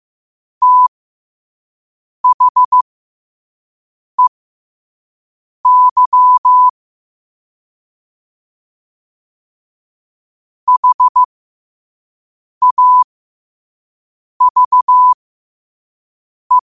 Morse code, heard as a keyed tone, about 15 wpm, spaced as for 5 wpm, 1000 Hz. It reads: THEY HAVE
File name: qotdfiles / 5wpm